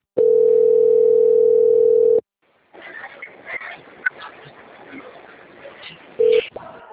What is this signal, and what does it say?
Ringback tone (North America), 440 + 480 Hz.